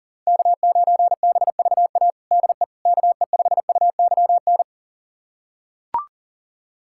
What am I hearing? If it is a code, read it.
Morse code, heard as a keyed tone, about 40 wpm, 700 Hz. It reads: K9BVA DE KE5UYN
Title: USA Callsign DE Callsign 40wpm